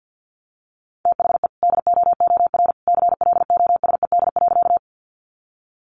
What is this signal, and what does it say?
Morse code, heard as a keyed tone, about 50 wpm, 700 Hz. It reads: THE DOOR CLOSED.
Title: Sentences from Top 1400 Words 50wpm